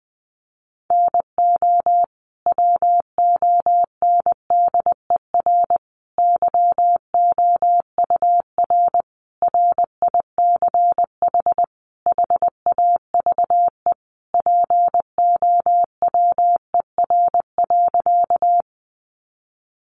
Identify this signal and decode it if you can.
Morse code, heard as a keyed tone, about 20 wpm, 700 Hz. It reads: NO WONDER YOUR RICH HAVE POWER.
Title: Sentences from Top 1100 Words 20wpm